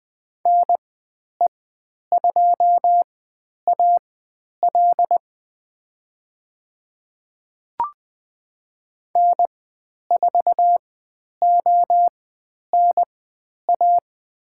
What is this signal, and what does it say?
Morse code, heard as a keyed tone, about 20 wpm, spaced as for 10 wpm, 700 Hz. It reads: NE2AL N4ONA
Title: 109 Lesson 18g - Letter m - Call Signs 20wpm